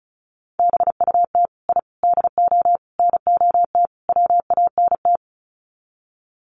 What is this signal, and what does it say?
Morse code, heard as a keyed tone, about 34 wpm, 700 Hz. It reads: BUT I DO NOT WANT
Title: Sets of 5 Words 35wpm